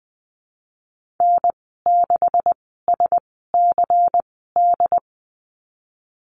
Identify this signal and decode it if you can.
Morse code, heard as a keyed tone, about 20 wpm, spaced as for 14 wpm, 700 Hz. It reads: N6SCD
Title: 219 Lesson 35g - Number 0 - Call Signs 20wpm